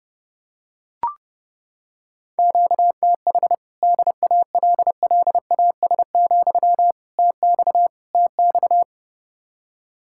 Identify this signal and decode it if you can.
Morse code, heard as a keyed tone, about 29 wpm, 700 Hz. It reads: QTH DALLAS, TX TX